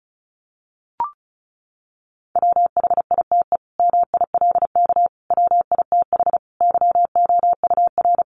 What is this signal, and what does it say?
Morse code, heard as a keyed tone, about 34 wpm, 700 Hz. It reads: WHITE MILK WITH YOUR
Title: Sentences from Top 1500 Words 35wpm